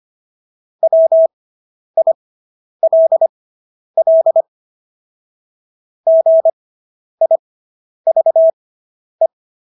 Morse code, heard as a keyed tone, about 25 wpm, spaced as for 10 wpm, 650 Hz. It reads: WILL GIVE